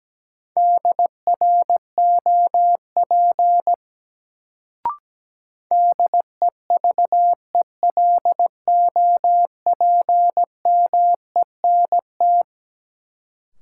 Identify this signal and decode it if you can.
Morse code, heard as a keyed tone, about 17 wpm, 700 Hz. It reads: DROP DEVELOPMENT
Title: Sentences from Top 1400 Words - New Word Drill Alt - 17wpm